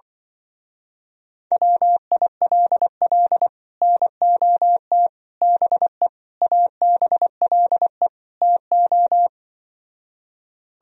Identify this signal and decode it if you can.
Morse code, heard as a keyed tone, about 24 wpm, 700 Hz. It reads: WILL NOT BE ABLE TO
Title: Speed-Racing - Sets of 5 Words 20wpm